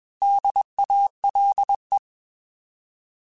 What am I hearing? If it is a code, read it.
Morse code, heard as a keyed tone, about 21 wpm, 800 Hz. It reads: DALE
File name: Alphabetical / Alphabetical mp3s / Ds @20